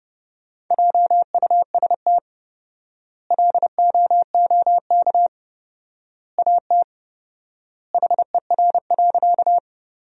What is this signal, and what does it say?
Morse code, heard as a keyed tone, about 32 wpm, 700 Hz. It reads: JUST LOOK AT HER.
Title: 4x Word-Spacing - Sentences from Top 200 Words - 30wpm